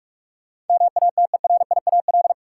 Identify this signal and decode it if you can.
Morse code, heard as a keyed tone, about 46 wpm, 700 Hz. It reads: MATERIAL
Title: Sentences from Top 1100 Words - New Word Drill Alt - 45wpm